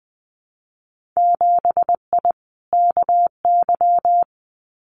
Morse code, heard as a keyed tone, about 20 wpm, 700 Hz. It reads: 7I KY
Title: Winter Field Day Exchanges Encore 20wpm